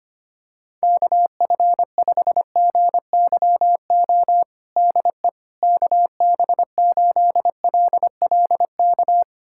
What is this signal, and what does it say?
Morse code, heard as a keyed tone, about 25 wpm, 700 Hz. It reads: KF5GYO DE KB8LLK
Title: USA Callsign DE Callsign 25wpm